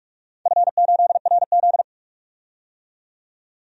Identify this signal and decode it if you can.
Morse code, heard as a keyed tone, about 44 wpm, 690 Hz. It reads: F8RZ
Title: DXCC-100 Call Signs - 4 Characters - 45wpm